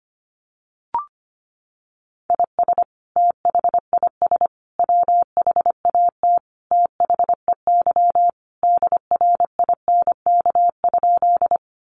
Morse code, heard as a keyed tone, about 25 wpm, 700 Hz. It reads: IS THIS WHAT THEY DRINK?
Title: Sentences from Top 1500 Words 25wpm